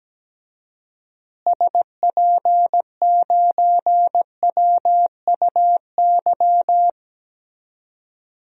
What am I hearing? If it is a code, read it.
Morse code, heard as a keyed tone, about 17 wpm, 700 Hz. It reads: SP9WUY
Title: DXCC-100 Call Signs - 6 Characters - Repeated 2x - 17wpm